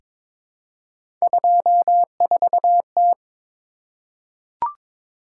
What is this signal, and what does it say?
Morse code, heard as a keyed tone, about 22 wpm, 700 Hz. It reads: 24T
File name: practice / 22wpm / Three Letter-Number Combinations - 22wpm